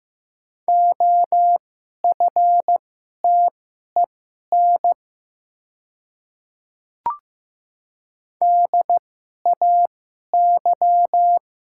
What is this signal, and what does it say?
Morse code, heard as a keyed tone, about 15 wpm, spaced as for 11 wpm, 700 Hz. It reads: OFTEN DAY
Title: 2x Character Spacing - Top 300 Words - 15wpm